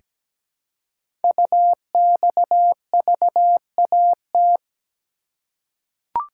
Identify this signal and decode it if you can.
Morse code, heard as a keyed tone, about 17 wpm, 700 Hz. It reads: UXVAT
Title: Code Groups - 5 Letters - 17wpm